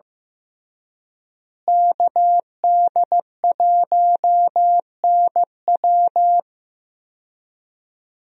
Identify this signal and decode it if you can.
Morse code, heard as a keyed tone, about 15 wpm, 700 Hz. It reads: KD1NW
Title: USA Call Signs - 5 Characters - Repeated 3x - 15wpm